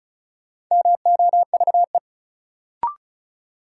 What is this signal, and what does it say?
Morse code, heard as a keyed tone, about 35 wpm, 700 Hz. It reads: MOVE
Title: Sentences from Top 200 Words - New Word Drill - 35wpm